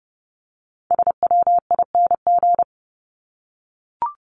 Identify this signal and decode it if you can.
Morse code, heard as a keyed tone, about 30 wpm, 700 Hz. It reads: SWING
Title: Top 1751-2000 Words 30wpm